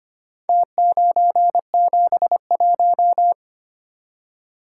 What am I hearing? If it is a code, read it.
Morse code, heard as a keyed tone, about 25 wpm, 700 Hz. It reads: T971
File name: practice / 25wpm / Four Letter-Number Combinations - 25wpm